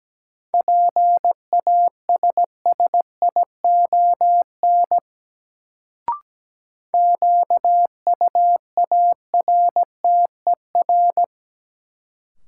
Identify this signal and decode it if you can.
Morse code, heard as a keyed tone, about 17 wpm, 700 Hz. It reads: PASSION QUARTER
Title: Sentences from Top 1300 Words - New Word Drill Alt - 17wpm